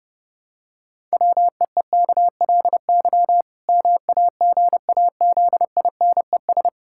Morse code, heard as a keyed tone, about 32 wpm, 700 Hz. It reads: WEEKLY MAGAZINES
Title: Pangram Challenge v2 - 30wpm